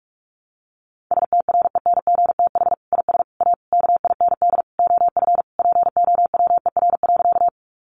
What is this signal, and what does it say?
Morse code, heard as a keyed tone, about 44 wpm, 700 Hz. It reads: STRENGTH IS A KIND OF POWER.
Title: Sentences from Top 1300 Words 45wpm